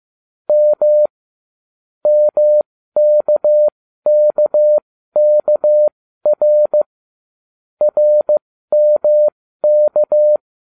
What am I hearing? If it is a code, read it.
Morse code, heard as a keyed tone, about 15 wpm, spaced as for 13 wpm, 600 Hz. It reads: M MKKKR RMK